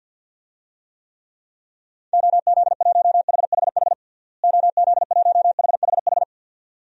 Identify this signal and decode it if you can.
Morse code, heard as a keyed tone, about 52 wpm, 700 Hz. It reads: OZ1HHH OZ1HHH
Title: DXCC-100 Call Signs - 6 Characters - Repeated 2x - 50wpm